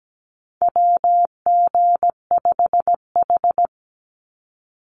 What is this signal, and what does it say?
Morse code, heard as a keyed tone, about 17 wpm, 700 Hz. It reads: WG5H
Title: Call Signs Repeated 3x - 17wpm